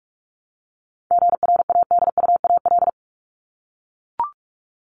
Morse code, heard as a keyed tone, about 44 wpm, 700 Hz. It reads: GRADUAL
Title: LY Suffix - Top 100 Words with Base Words 45wpm